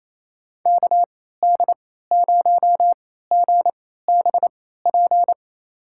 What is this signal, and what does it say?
Morse code, heard as a keyed tone, about 28 wpm, spaced as for 16 wpm, 700 Hz. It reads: KD0GBP
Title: USA Call Signs - 6 Characters - 3x Character Spacing - 28wpm